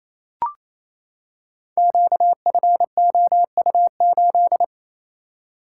Morse code, heard as a keyed tone, about 27 wpm, 700 Hz. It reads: QFOU8
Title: Code Groups - 5 Letter:Numbers - 28wpm